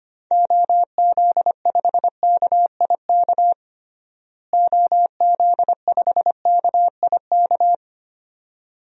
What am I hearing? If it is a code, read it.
Morse code, heard as a keyed tone, about 25 wpm, 700 Hz. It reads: OZ5KIK OZ5KIK
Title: DXCC-100 Call Signs - 6 Characters - Repeated 2x - 25wpm